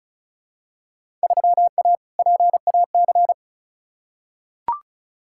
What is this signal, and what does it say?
Morse code, heard as a keyed tone, about 36 wpm, 700 Hz. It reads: 3A PAC
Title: Field Day Exchanges Encore - 35wpm